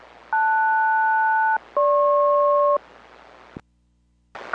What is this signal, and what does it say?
Aircraft SELCAL, heard as two pairs of tones. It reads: LS-GP